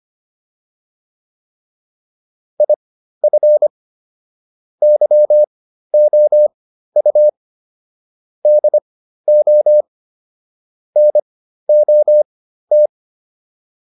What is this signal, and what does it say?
Morse code, heard as a keyed tone, about 25 wpm, spaced as for 13 wpm, 600 Hz. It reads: IF YOU DO NOT
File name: WD302 13